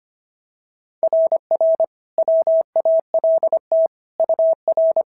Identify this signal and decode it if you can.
Morse code, heard as a keyed tone, about 25 wpm, 690 Hz. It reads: RR WALT UR
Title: SOTA QSO Practice 25wpm